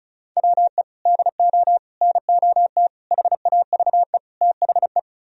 Morse code, heard as a keyed tone, about 44 wpm, 700 Hz. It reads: WE DO NOT HAVE THE